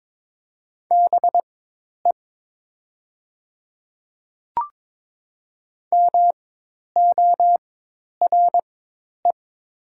Morse code, heard as a keyed tone, about 22 wpm, spaced as for 10 wpm, 700 Hz. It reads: BE MORE
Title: 4x Character Spacing - Top 100 Words - 22wpm